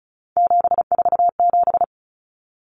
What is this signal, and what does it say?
Morse code, heard as a keyed tone, about 34 wpm, 700 Hz. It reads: 747
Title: Common Cut Numbers - 3-Digit 35wpm